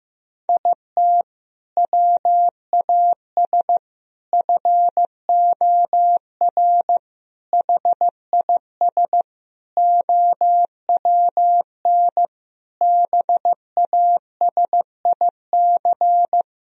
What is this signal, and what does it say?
Morse code, heard as a keyed tone, about 15 wpm, 700 Hz. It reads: IT WAS FOR HIS OWN BASIC
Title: Sentences from Top 1000 Words - Review Part 3 - 15wpm